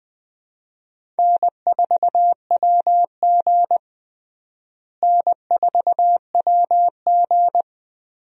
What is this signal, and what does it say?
Morse code, heard as a keyed tone, about 20 wpm, 700 Hz. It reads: N4WG N4WG
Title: USA Call Signs - 4 Characters - Repeated 2x - 20wpm